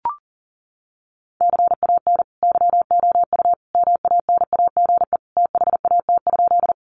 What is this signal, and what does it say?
Morse code, heard as a keyed tone, about 40 wpm, 700 Hz. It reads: CAN YOU MANAGE THAT?
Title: Sentences from Top 1400 Words 40wpm